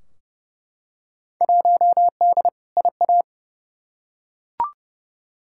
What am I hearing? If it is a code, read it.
Morse code, heard as a keyed tone, about 30 wpm, 700 Hz. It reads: 1D IA
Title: Field Day Exchanges - 30wpm